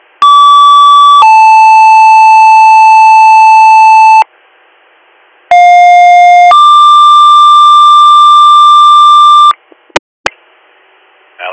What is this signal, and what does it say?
Two-tone page, heard as a one-second tone, then a three-second tone, about 1150 then 880 Hz, about 710 then 1180 Hz, with speech. It is heard from 0.2 to 9.5 s.